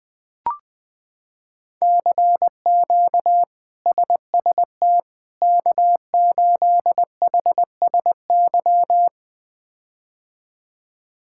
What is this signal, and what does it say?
Morse code, heard as a keyed tone, about 20 wpm, 700 Hz. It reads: CQ SST K8HSY
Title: K1USN SST QSO Practice - 20wpm